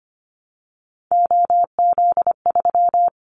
Morse code, heard as a keyed tone, about 25 wpm, 700 Hz. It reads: OZ3
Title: Callsign Prefixes for Top 100 DXCC - 25wpm